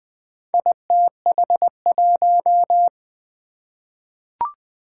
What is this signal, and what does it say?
Morse code, heard as a keyed tone, about 20 wpm, 700 Hz. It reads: ITH1
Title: Four Challenging Letter-Number Combinations 20wpm